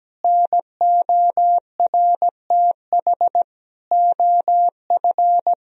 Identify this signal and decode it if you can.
Morse code, heard as a keyed tone, about 17 wpm, 700 Hz. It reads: NORTH OF
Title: Sentences from Top 1000 Words - Review Part 4 - 17wpm